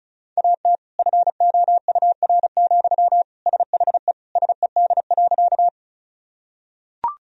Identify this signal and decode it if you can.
Morse code, heard as a keyed tone, about 36 wpm, 700 Hz. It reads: AT FOUR, SHE SED.
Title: Sentences from Top 300 Words and Abbreviations 35wpm